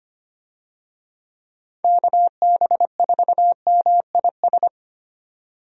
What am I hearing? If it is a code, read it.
Morse code, heard as a keyed tone, about 25 wpm, 700 Hz. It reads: KB4MIS
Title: USA Call Signs - 6 Characters - 25wpm